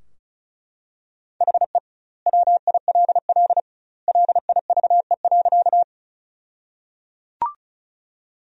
Morse code, heard as a keyed tone, about 34 wpm, 700 Hz. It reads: HE WILL LIVE.
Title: 2x Word-Spacing - Sentences from Top 200 Words - 35wpm